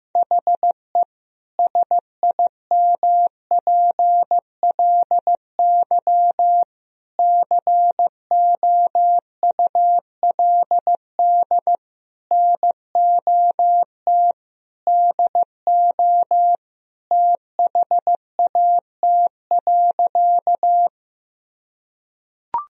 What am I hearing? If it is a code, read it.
Morse code, heard as a keyed tone, about 16 wpm, 700 Hz. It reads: HE SIMPLY COULD NOT DO THAT.
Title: Sentences from Top 1000 Words - Review Part 4 - 15wpm